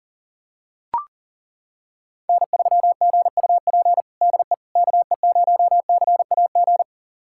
Morse code, heard as a keyed tone, about 37 wpm, 700 Hz. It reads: N3GUP DE KE0CAG
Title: USA Callsign DE Callsign 40wpm